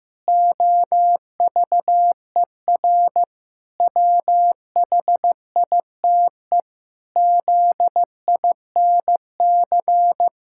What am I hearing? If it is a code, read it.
Morse code, heard as a keyed tone, about 15 wpm, 700 Hz. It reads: OVER WHITE ZINC
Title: Pangram Challenge v2 - 15wpm